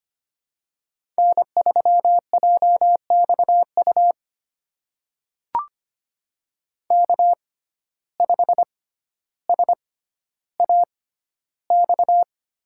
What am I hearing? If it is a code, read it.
Morse code, heard as a keyed tone, about 25 wpm, spaced as for 13 wpm, 700 Hz. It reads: N3JXU K5SAX
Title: USA Call Signs - 5 Characters - 6x Character Spacing - 25wpm